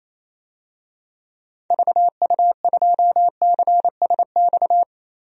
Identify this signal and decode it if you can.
Morse code, heard as a keyed tone, about 28 wpm, 700 Hz. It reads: VU2CSX
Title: DXCC-100 Call Signs - 6 Characters - 28wpm